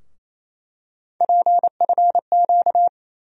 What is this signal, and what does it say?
Morse code, heard as a keyed tone, about 28 wpm, 700 Hz. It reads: PFQ
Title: USA Call Sign Postfixes - 3 Letters 28wpm